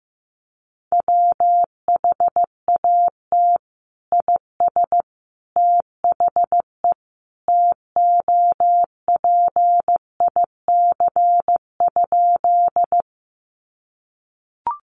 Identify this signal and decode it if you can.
Morse code, heard as a keyed tone, about 15 wpm, 700 Hz. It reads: WHAT IS THE TOPIC?
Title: Sentences from Top 1200 Words 15wpm